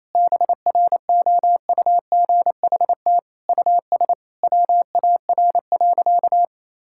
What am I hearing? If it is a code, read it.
Morse code, heard as a keyed tone, about 28 wpm, 700 Hz. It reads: BROUGHT US WAR.
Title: Sentences from Top 1000 Words - Review Part 6 - 28wpm